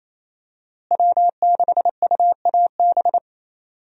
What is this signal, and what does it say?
Morse code, heard as a keyed tone, about 28 wpm, 700 Hz. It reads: W6UAB
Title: USA Call Signs - 5 Characters - Repeated 2x - 28wpm